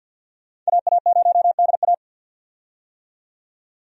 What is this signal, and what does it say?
Morse code, heard as a keyed tone, about 51 wpm, 700 Hz. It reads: AA0DA
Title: USA Call Signs - 5 Characters - 50wpm